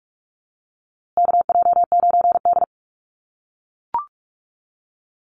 Morse code, heard as a keyed tone, about 44 wpm, 690 Hz. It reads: KJ9D
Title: USA Call Signs Repeated 3x 45wpm